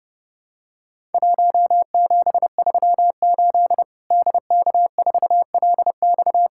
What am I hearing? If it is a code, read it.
Morse code, heard as a keyed tone, about 30 wpm, 700 Hz. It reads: 1738 DK4LX
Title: Worked All Europe DX Contest - QTC Exchange - 30wpm